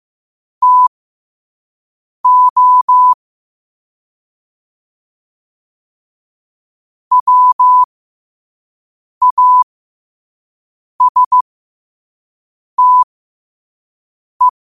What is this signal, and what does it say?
Morse code, heard as a keyed tone, about 15 wpm, spaced as for 5 wpm, 1000 Hz. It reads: TO WASTE